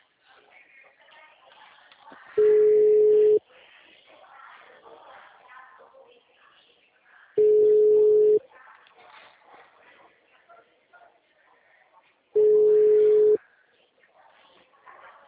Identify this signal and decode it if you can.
Ringback tone (Europe, 425 Hz), with speech.